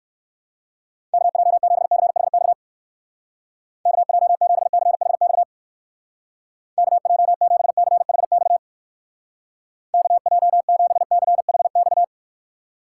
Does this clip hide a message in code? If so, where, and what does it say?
Yes: Morse code from 1.1 to 12.0 s, heard as a keyed tone, about 59 wpm, 700 Hz. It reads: KJ7CHX KJ7CHX KJ7CHX KJ7CHX